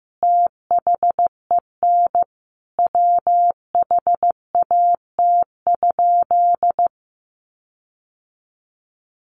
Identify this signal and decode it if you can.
Morse code, heard as a keyed tone, about 14 wpm, 700 Hz. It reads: THEN WHAT?
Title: Sentences from Top 1500 Words 15wpm